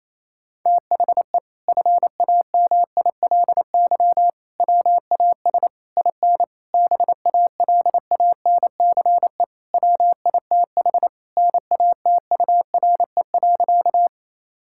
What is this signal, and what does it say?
Morse code, heard as a keyed tone, about 28 wpm, 700 Hz. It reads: THE FAMILY WAS IN BALANCE WITH NATURE.
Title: Sentences from Top 1000 Words - Review Part 6 - 28wpm